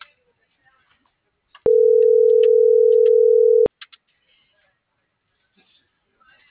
Ringback tone (North America), 440 + 480 Hz.